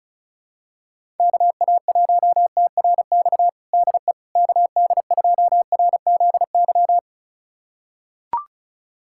Morse code, heard as a keyed tone, about 36 wpm, 700 Hz. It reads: KA1TRX DE KD2RZY
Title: USA Callsign DE Callsign - 6 Characters - 35wpm